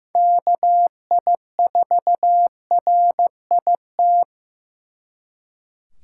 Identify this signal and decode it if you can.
Morse code, heard as a keyed tone, about 18 wpm, 700 Hz. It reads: KI4RIT
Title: Speed Racing - USA 2x3 Call Signs 15wpm